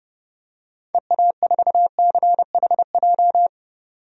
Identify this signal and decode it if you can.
Morse code, heard as a keyed tone, about 30 wpm, 700 Hz. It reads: EA4CHJ
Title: DXCC-100 Call Signs - 6 Characters - 30wpm